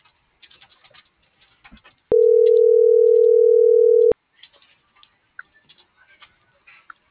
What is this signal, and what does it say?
Ringback tone (North America), 440 + 480 Hz.